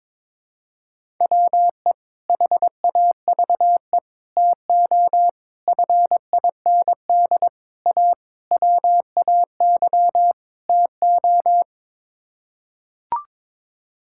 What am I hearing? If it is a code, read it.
Morse code, heard as a keyed tone, about 22 wpm, 700 Hz. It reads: WE HAVE TO FIND A WAY TO
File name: practice / 22wpm / Sets of 7 Words - 22wpm